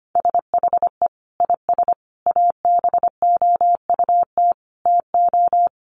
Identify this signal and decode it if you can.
Morse code, heard as a keyed tone, about 25 wpm, 700 Hz. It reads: SHE IS ABOUT TO
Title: Sentences from Top 1300 Words 25wpm